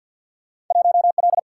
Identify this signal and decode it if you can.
Morse code, heard as a keyed tone, about 53 wpm, 700 Hz. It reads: 1L
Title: Single-Letter Single-Number 50wpm